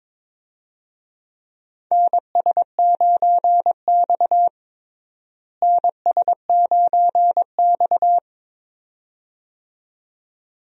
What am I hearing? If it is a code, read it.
Morse code, heard as a keyed tone, about 22 wpm, 700 Hz. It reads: NS9X NS9X
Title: USA Call Signs - 4 Characters - Repeated 2x - 22wpm